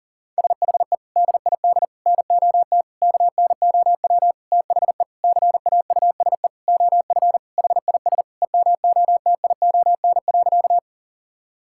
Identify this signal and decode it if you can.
Morse code, heard as a keyed tone, about 40 wpm, 700 Hz. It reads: SHE DID NOT KNOW THE CAUSE OF HIS EMOTION.
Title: Sentences from Top 1600 Words 40wpm